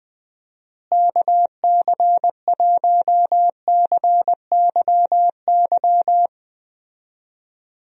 Morse code, heard as a keyed tone, about 20 wpm, 700 Hz. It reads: KC1CYY
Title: USA Call Signs - 6 Characters - Repeated 2x - 20wpm